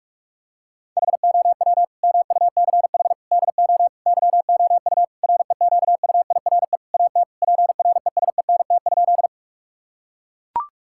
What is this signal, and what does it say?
Morse code, heard as a keyed tone, about 42 wpm, 700 Hz. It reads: HOW MUCH DO YOU REQUIRE AT PRESENT?
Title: Sentences from Top 1000 Words - Review Part 6 - 45wpm